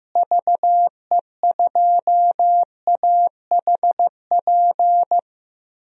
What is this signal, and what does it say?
Morse code, heard as a keyed tone, about 15 wpm, 700 Hz. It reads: VE2AHP
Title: DXCC-100 Call Signs - 6 Characters - Repeated 2x - 15wpm